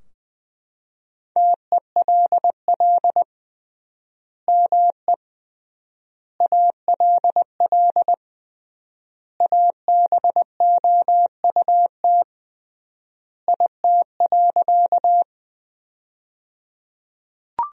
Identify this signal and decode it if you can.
Morse code, heard as a keyed tone, about 20 wpm, 700 Hz. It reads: TELL ME ALL ABOUT IT.
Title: 3x Word-Spacing - Sentences from Top 200 Words - 20wpm